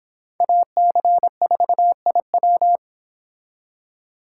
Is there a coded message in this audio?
Yes: Morse code from 0.4 to 2.8 s, heard as a keyed tone, about 26 wpm, 700 Hz. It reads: AC4IW